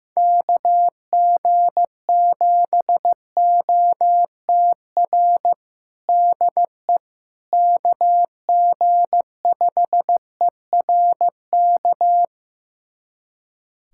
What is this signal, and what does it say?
Morse code, heard as a keyed tone, about 15 wpm, 700 Hz. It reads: KG7OTR DE KG5ERK
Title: USA Callsign DE Callsign - 6 Characters - 15wpm